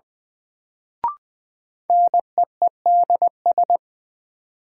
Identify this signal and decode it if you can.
Morse code, heard as a keyed tone, about 20 wpm, 700 Hz. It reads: NEEDS
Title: Sentences from Top 1000 Words - Review Part 3 - New Word Drill - 20wpm